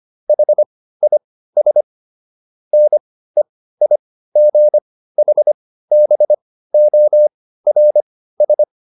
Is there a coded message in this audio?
Yes: Morse code from 0.3 to 8.6 s, heard as a keyed tone, about 25 wpm, spaced as for 15 wpm, 600 Hz. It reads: HIS NEIGHBORS